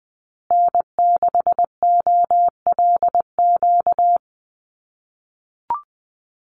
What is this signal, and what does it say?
Morse code, heard as a keyed tone, about 20 wpm, 700 Hz. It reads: N6OLQ
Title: USA Call Signs Repeated 3x 20wpm